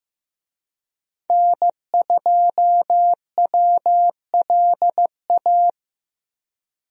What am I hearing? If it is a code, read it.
Morse code, heard as a keyed tone, about 15 wpm, 700 Hz. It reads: N2WLA
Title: USA Call Signs - 5 Characters - Repeated 2x - 15wpm